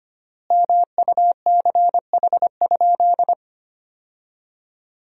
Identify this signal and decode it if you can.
Morse code, heard as a keyed tone, about 25 wpm, 700 Hz. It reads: MUCH?
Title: 3x Word-Spacing - Sentences from Top 400 Words - 25wpm